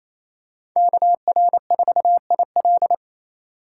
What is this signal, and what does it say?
Morse code, heard as a keyed tone, about 28 wpm, 700 Hz. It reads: KR4IL
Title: USA Call Signs - 5 Characters - 6x Character Spacing - 28wpm